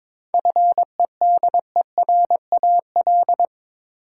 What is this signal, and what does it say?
Morse code, heard as a keyed tone, about 22 wpm, 700 Hz. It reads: FEDERAL
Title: Sentences from Top 1000 Words - Review Part 5 - New Word Drill - 22wpm